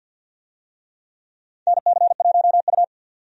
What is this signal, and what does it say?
Morse code, heard as a keyed tone, about 47 wpm, 700 Hz. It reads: NC1U
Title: USA Call Signs - 4 Characters - 50wpm